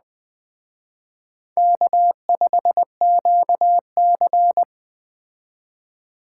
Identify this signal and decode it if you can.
Morse code, heard as a keyed tone, about 21 wpm, 700 Hz. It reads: K5QC
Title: Speed Racing - USA Call Signs 17wpm